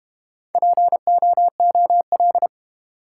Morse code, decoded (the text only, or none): POOL